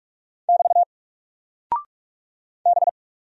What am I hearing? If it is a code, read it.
Morse code, heard as a keyed tone, about 43 wpm, 700 Hz. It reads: <BT> B
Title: Single Character (Letter:Number:Punctuation:Prosign) Alt 45wpm